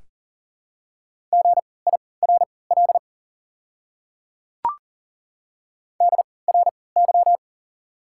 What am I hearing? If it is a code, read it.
Morse code, heard as a keyed tone, about 40 wpm, spaced as for 23 wpm, 690 Hz. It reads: GIRL DRY
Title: 3x Character Spacing - Top 500 Words - 40wpm